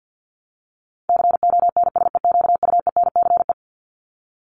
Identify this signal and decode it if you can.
Morse code, heard as a keyed tone, about 50 wpm, 700 Hz. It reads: CONSEQUENCE
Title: LY Suffix - Top 100 Words with Base Words 50wpm